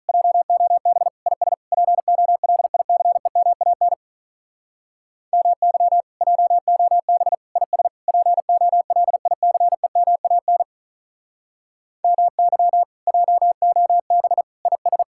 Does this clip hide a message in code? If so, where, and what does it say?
Yes: Morse code from 0.1 to 15.0 s, heard as a keyed tone, about 46 wpm, 700 Hz. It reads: JOB IS POLICEMAN MY JOB IS POLICEMAN MY JOB IS